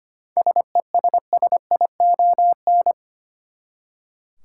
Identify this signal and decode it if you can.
Morse code, heard as a keyed tone, about 25 wpm, 700 Hz. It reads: SESSION
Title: Seven Letter Words - 25wpm